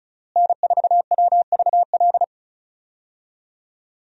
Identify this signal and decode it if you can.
Morse code, heard as a keyed tone, about 36 wpm, 700 Hz. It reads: N4WVL
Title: USA Call Signs - 5 Characters - Repeated 2x - 35wpm